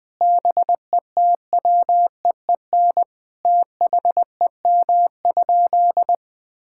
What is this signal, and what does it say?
Morse code, heard as a keyed tone, about 21 wpm, 700 Hz. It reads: BETWEEN THEM?
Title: Sentences from Top 1000 Words - Review Part 5 - 20wpm